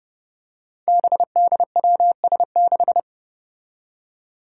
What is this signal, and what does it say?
Morse code, heard as a keyed tone, about 30 wpm, 700 Hz. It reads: BDWS6
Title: Five Challenging Letter-Number Combinations 30wpm